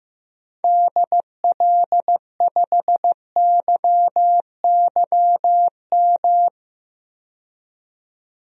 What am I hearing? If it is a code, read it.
Morse code, heard as a keyed tone, about 15 wpm, 700 Hz. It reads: DL5YYM
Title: DXCC-100 Call Signs - 6 Characters - 15wpm